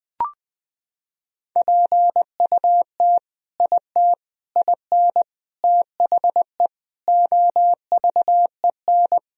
Morse code, heard as a keyed tone, about 21 wpm, 700 Hz. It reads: PUT IT IN THE OVEN
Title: Sentences from Top 1000 Words - Review Part 4 - 20wpm